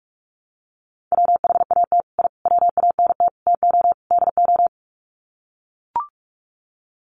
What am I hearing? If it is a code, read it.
Morse code, heard as a keyed tone, about 44 wpm, 700 Hz. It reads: WHAT I WANT TO DO
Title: Sets of 5 Words Encore 45wpm